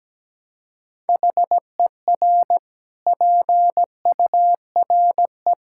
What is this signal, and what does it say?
Morse code, heard as a keyed tone, about 17 wpm, 700 Hz. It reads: HER PURE
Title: Sentences from Top 1600 Words 17wpm